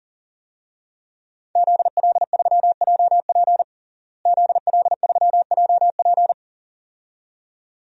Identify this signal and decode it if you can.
Morse code, heard as a keyed tone, about 38 wpm, 700 Hz. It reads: ZL3JP ZL3JP
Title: DXCC-100 Call Signs - 5 Characters - Repeated 2x - 40wpm